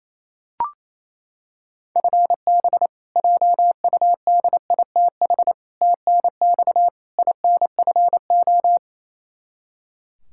Morse code, heard as a keyed tone, about 27 wpm, 700 Hz. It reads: FB JUDITH TNX INFO
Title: K1USN SST QSO Practice 28wpm